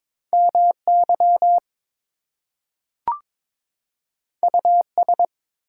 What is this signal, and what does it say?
Morse code, heard as a keyed tone, about 22 wpm, 700 Hz. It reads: MY US
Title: Two Letter Words - 22wpm